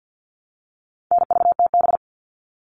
Morse code, heard as a keyed tone, about 48 wpm, 700 Hz. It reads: NVTB
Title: All Cut Numbers - 4-Digit 50wpm